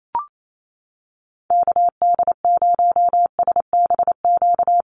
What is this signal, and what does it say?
Morse code, heard as a keyed tone, about 28 wpm, 700 Hz. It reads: KD0SBQ
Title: USA Call Signs Repeated 3x 28wpm